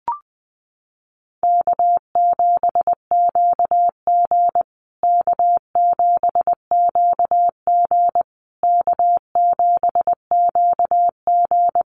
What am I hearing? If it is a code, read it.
Morse code, heard as a keyed tone, about 20 wpm, 700 Hz. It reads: K7QG K7QG K7QG
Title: USA Call Signs Repeated 3x 20wpm